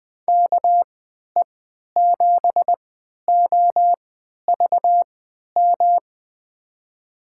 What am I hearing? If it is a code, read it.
Morse code, heard as a keyed tone, about 20 wpm, spaced as for 11 wpm, 700 Hz. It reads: KE7OVM
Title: USA Call Signs - 6 Characters - 3x Character Spacing - 20wpm